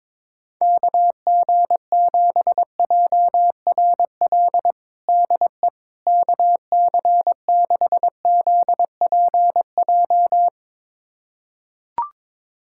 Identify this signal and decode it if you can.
Morse code, heard as a keyed tone, about 22 wpm, 700 Hz. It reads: KG7JRL DE KC6ZPJ
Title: USA Callsign DE Callsign - 6 Characters - 22wpm